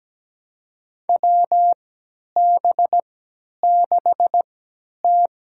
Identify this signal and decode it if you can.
Morse code, heard as a keyed tone, about 17 wpm, spaced as for 10 wpm, 700 Hz. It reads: WB6T
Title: USA Call Signs - 3x Character Spacing - 17wpm